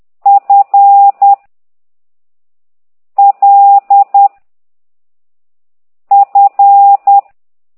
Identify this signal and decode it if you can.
Morse code, heard as a keyed tone, about 10 wpm, 800 Hz. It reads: F L F
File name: ct2idw / exerc8